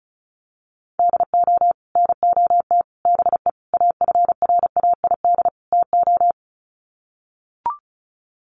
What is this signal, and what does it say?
Morse code, heard as a keyed tone, about 34 wpm, 700 Hz. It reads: DO NOT BE AFRAID TO
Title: Sets of 5 Words Encore2 35wpm